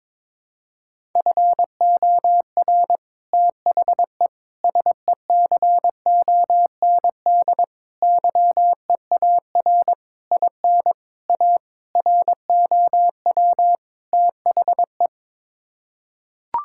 Morse code, heard as a keyed tone, about 22 wpm, 700 Hz. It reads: FOR THE SECOND YEAR IN A ROW THE
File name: practice / 22wpm / Sets of 8 Words - 22wpm